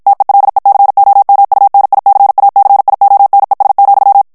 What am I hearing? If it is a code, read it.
Morse code, heard as a keyed tone, about 56 wpm, 800 Hz. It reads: TELECOMUNICACIONES,
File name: morse mp3 / ESPAGNOL TEXTE EN CLAIR 2 VITESSE 50 MOTS MINUTES